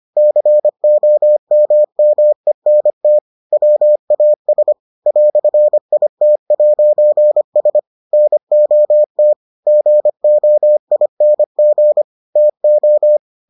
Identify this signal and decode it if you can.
Morse code, heard as a keyed tone, about 25 wpm, 600 Hz. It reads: COMMENT WAS "IT'S NOT GOING TO